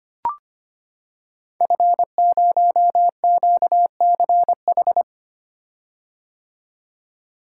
Morse code, heard as a keyed tone, about 25 wpm, 700 Hz. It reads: F0QCH
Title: Code Groups - 5 Letter:Numbers - 25wpm